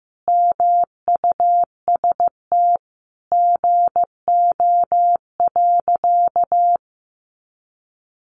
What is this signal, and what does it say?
Morse code, heard as a keyed tone, about 15 wpm, 700 Hz. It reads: MUST GO.
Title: Sentences from Top 1200 Words 15wpm